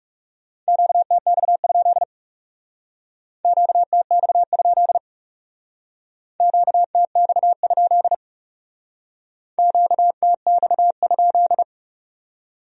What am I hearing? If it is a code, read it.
Morse code, heard as a keyed tone, about 36 wpm, 700 Hz. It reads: QTX? QTX? QTX? QTX?
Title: Speed-Racing Q-Codes - 30wpm